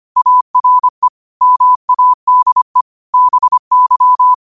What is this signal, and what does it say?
Morse code, heard as a keyed tone, about 25 wpm, 1000 Hz. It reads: ARE MADE BY
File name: 25wpm